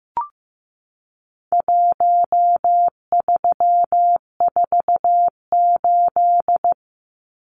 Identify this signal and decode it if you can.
Morse code, heard as a keyed tone, about 15 wpm, 700 Hz. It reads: 1348
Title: Numbers - Contest Style - 4-Digit 15wpm